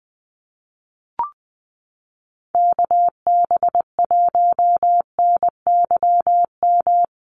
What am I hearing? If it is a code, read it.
Morse code, heard as a keyed tone, about 20 wpm, 700 Hz. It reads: KB1NYM
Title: USA Call Signs Repeated 2x 20wpm